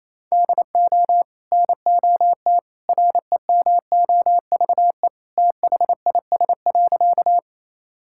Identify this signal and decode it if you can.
Morse code, heard as a keyed tone, about 28 wpm, 700 Hz. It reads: DO NOT REMOVE THIS.
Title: Sentences from Top 1000 Words - Review Part 5 - 28wpm